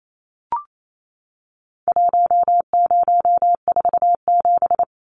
Morse code, heard as a keyed tone, about 28 wpm, 700 Hz. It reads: 1047
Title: Numbers - Contest Style - 4-Digit 28wpm